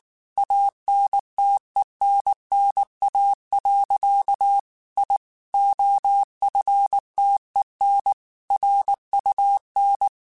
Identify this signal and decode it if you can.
Morse code, heard as a keyed tone, about 19 wpm, 800 Hz. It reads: ANTENNA. I OFTEN RUN
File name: QSOs / QSOs-mp3s / WA7QOX 2@18